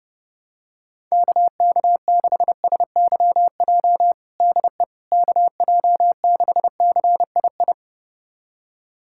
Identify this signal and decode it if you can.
Morse code, heard as a keyed tone, about 30 wpm, 700 Hz. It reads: KK6SYJ DE KJ6CII
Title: USA Callsign DE Callsign - 6 Characters - 30wpm